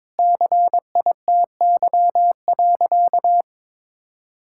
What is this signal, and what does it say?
Morse code, heard as a keyed tone, about 23 wpm, 700 Hz. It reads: CITY.
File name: practice / 22wpm / Sentences with 2-Word Phrases - 22wpm